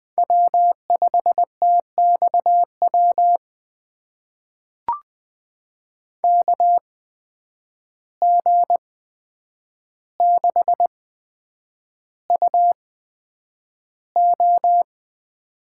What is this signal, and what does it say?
Morse code, heard as a keyed tone, about 20 wpm, spaced as for 9 wpm, 700 Hz. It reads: W5TXW KG6UO
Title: USA Call Signs - 5 Characters - 8x Character Spacing - 20wpm